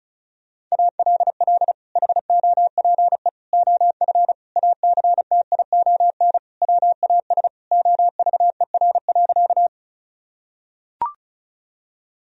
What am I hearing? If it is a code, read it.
Morse code, heard as a keyed tone, about 36 wpm, 700 Hz. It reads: ALL HOPE OF ACTION WAS OVER.
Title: Sentences from Top 1000 Words - Review Part 4 - 35wpm